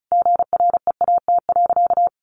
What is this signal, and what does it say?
Morse code, heard as a keyed tone, about 34 wpm, 700 Hz. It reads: GREAT.
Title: Sentences from Top 1400 Words 35wpm